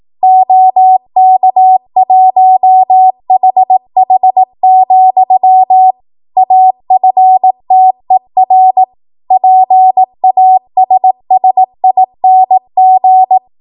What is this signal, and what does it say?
Morse code, heard as a keyed tone, about 18 wpm, 750 Hz. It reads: OK1HH, AFTER PASSING
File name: Archive / 18 WPM / 18WPM